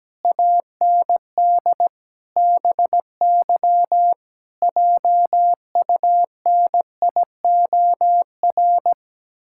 Morse code, heard as a keyed tone, about 17 wpm, 700 Hz. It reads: AND BY JUNIOR
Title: Pangram Challenge v2 - 17wpm